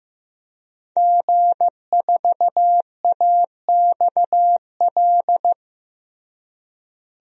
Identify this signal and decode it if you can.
Morse code, heard as a keyed tone, about 15 wpm, 700 Hz. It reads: G4AXL
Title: DXCC-100 Call Signs - 5 Characters - Repeated 3x - 15wpm